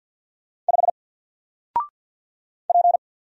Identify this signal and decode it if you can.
Morse code, heard as a keyed tone, about 49 wpm, 700 Hz. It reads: 5 P
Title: Single Character (Letter:Number:Punctuation:Prosign) Alt 50wpm